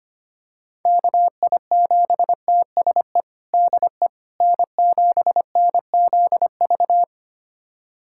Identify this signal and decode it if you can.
Morse code, heard as a keyed tone, about 25 wpm, 700 Hz. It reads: KI7TSE DE N7NZV
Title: USA Callsign DE Callsign 25wpm